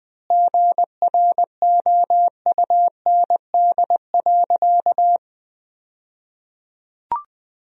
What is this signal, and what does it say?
Morse code, heard as a keyed tone, about 20 wpm, 700 Hz. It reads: GROUND.
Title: 2x Word-Spacing - Sentences from Top 500 Words - 20wpm